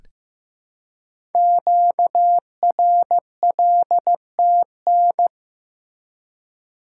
Morse code, heard as a keyed tone, about 15 wpm, 700 Hz. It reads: QRLTN
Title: Code Groups - 5 Letters - 15wpm